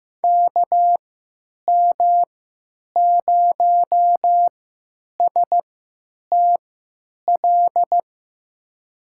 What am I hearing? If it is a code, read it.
Morse code, heard as a keyed tone, about 15 wpm, spaced as for 9 wpm, 700 Hz. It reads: KM0STL
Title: USA Call Signs - 6 Characters - 3x Character Spacing - 15wpm